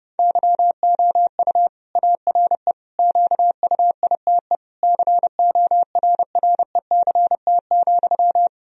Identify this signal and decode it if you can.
Morse code, heard as a keyed tone, about 30 wpm, 700 Hz. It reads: YOU ARE QUITE CORRECT,
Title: Sentences from Top 1000 Words - Review Part 2 - 30wpm